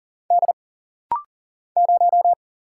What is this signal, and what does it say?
Morse code, heard as a keyed tone, about 40 wpm, 700 Hz. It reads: D 0
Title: Single Letter-Number Alt - Rapid-Fire 40wpm